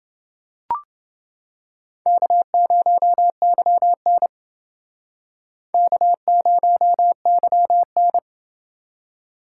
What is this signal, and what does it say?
Morse code, heard as a keyed tone, about 28 wpm, 700 Hz. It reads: K0YN K0YN